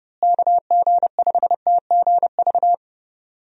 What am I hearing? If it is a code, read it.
Morse code, heard as a keyed tone, about 30 wpm, 700 Hz. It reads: KG5TGV
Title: USA Call Signs - 6 Characters - Repeated 3x - 30wpm